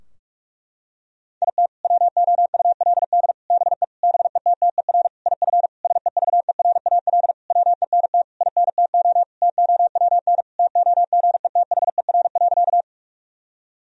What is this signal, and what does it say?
Morse code, heard as a keyed tone, about 46 wpm, 700 Hz. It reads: IT WOULD BE BETTER IF SEVERAL WENT INTO TOWN TOGETHER.
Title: Sentences from Top 1000 Words - Review Part 2 - 45wpm